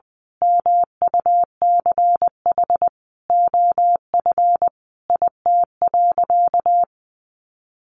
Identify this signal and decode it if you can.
Morse code, heard as a keyed tone, about 20 wpm, 700 Hz. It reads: MUCH OF IT.
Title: Sentences from Top 1400 Words 20wpm